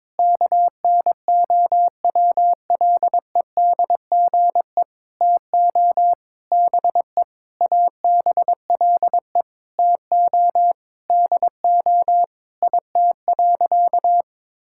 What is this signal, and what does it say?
Morse code, heard as a keyed tone, about 22 wpm, 700 Hz. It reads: KNOWLEDGE TO BE ABLE TO DO IT.
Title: Sentences from Top 1000 Words - Review Part 6 - 22wpm